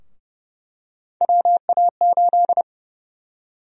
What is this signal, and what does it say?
Morse code, heard as a keyed tone, about 30 wpm, 700 Hz. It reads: WA8
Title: USA Call Sign Prefixes - 30wpm